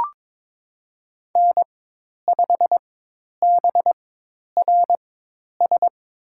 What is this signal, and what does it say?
Morse code, heard as a keyed tone, about 22 wpm, spaced as for 10 wpm, 700 Hz. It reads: N5BRS
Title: USA Call Signs - 4x Character Spacing - 22wpm